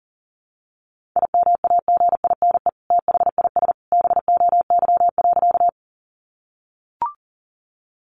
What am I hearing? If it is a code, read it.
Morse code, heard as a keyed tone, about 40 wpm, 700 Hz. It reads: IMAGINE THIS BOY.
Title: Sentences from Top 1400 Words 40wpm